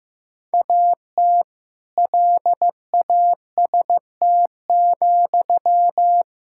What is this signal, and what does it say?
Morse code, heard as a keyed tone, about 15 wpm, 700 Hz. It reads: AT LAST,
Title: Sentences from Top 1000 Words - Review Part 5 - 15wpm